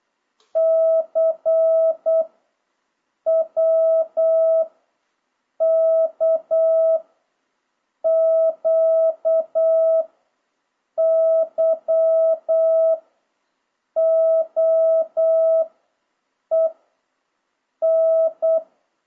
Morse code, heard as a keyed tone, about 8 wpm, 650 Hz. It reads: C W K Q Y O E N
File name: jh8cbh / 2026cw